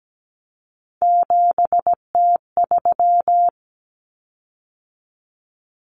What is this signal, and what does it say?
Morse code, heard as a keyed tone, about 17 wpm, 700 Hz. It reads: 7T3
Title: Common Cut Numbers - 3-Digit 17wpm